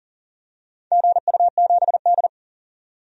Morse code, heard as a keyed tone, about 40 wpm, 700 Hz. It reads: GU7D
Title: DXCC-100 Call Signs - 4 Characters - 40wpm